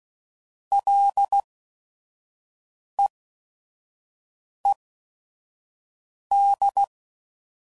Morse code, heard as a keyed tone, about 16 wpm, spaced as for 5 wpm, 800 Hz. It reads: LEED
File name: Prefixes / FLs & THs / Single L's